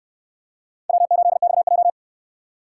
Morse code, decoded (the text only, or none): W8XJ